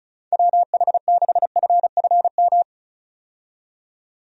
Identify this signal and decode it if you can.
Morse code, heard as a keyed tone, about 36 wpm, 700 Hz. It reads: WH6FFM
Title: USA Call Signs - 6 Characters - 35wpm